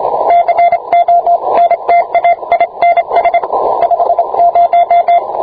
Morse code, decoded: CONTAINS 50